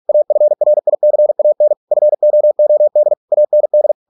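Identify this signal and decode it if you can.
Morse code, heard as a keyed tone, about 46 wpm, 600 Hz. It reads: AFRICAN FOOD AND
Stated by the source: Morse practice at 45 wpm, the novel Never by Ken Follett in English, from CW med Gnister.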